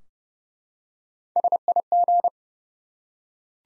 Morse code, heard as a keyed tone, about 30 wpm, 700 Hz. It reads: SIG
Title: Top 90 Abbr 30WPM